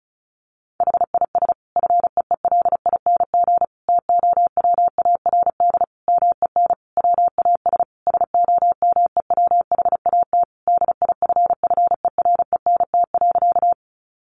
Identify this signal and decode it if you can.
Morse code, heard as a keyed tone, about 34 wpm, 700 Hz. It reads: HIS FEELING TOWARD MEN WAS SOMEWHAT DIFFERENT.
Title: Sentences from Top 1300 Words 35wpm